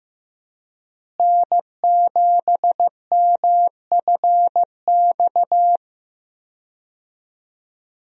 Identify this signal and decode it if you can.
Morse code, heard as a keyed tone, about 15 wpm, 700 Hz. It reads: N7MFX
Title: USA Call Signs - 5 Characters - Repeated 2x - 15wpm